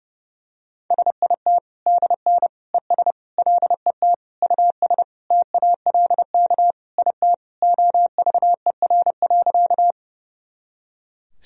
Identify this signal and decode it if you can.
Morse code, heard as a keyed tone, about 31 wpm, 700 Hz. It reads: SIT DN ES LET US TALK IT OVER.
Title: Sentences from Top 500 Words and Abbreviations 30wpm